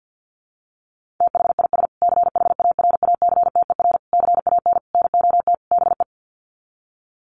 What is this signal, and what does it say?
Morse code, heard as a keyed tone, about 50 wpm, 700 Hz. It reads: THIS CHARACTER CAN NOT BE
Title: Sets of 5 Words Encore2 50wpm